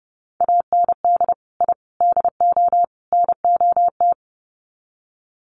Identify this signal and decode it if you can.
Morse code, heard as a keyed tone, about 30 wpm, 700 Hz. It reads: AND I DO NOT
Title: Sets of 4 Words 30wpm